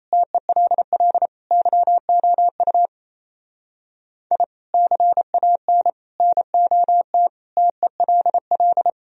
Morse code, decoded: TELL YOU I CAN NOT TELL

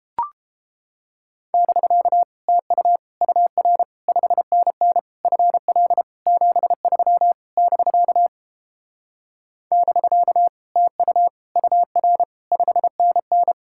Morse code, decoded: <BK> TU UR 5NN FL 73 <BK> <BK> TU UR 5NN

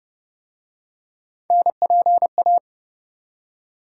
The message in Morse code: NPA

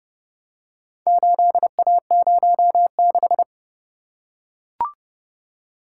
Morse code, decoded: 8A06

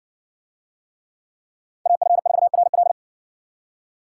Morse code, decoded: AF4RL

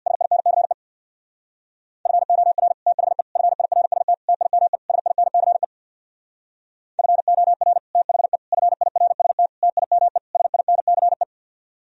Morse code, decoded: SINCE FOR THE FIRST TIME SINCE FOR THE FIRST TIME SINCE

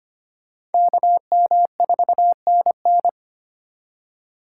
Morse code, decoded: KM4NN

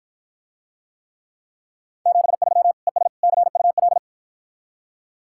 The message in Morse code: 73 ES CUL

